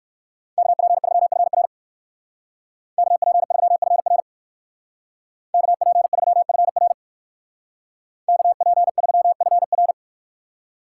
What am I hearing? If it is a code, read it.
Morse code, heard as a keyed tone, about 59 wpm, 700 Hz. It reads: KP3FR KP3FR KP3FR KP3FR